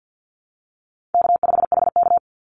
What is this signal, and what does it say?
Morse code, heard as a keyed tone, about 50 wpm, 700 Hz. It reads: K5HK
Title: Call Signs Repeated 2x - 50wpm